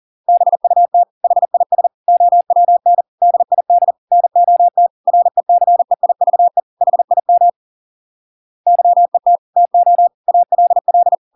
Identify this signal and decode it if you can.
Morse code, heard as a keyed tone, about 40 wpm, 700 Hz. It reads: BUT HIS OWN DID NOT RECEIVE HIM YET TO ALL